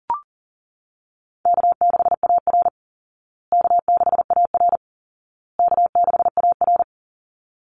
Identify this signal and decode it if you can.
Morse code, heard as a keyed tone, about 37 wpm, 700 Hz. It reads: K6AR K6AR K6AR
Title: Easy Call Signs Repeated 3x - 40wpm